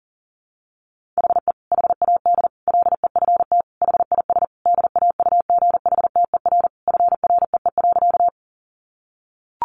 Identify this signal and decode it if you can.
Morse code, heard as a keyed tone, about 40 wpm, 700 Hz. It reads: HE HAD LEFT HIS DAUGHTER FREE.
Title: Sentences from Top 1500 Words 40wpm